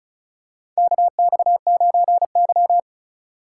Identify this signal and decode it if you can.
Morse code, heard as a keyed tone, about 36 wpm, 700 Hz. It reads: KX9Y
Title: USA Call Signs - 4 Characters - 3x Character Spacing - 35wpm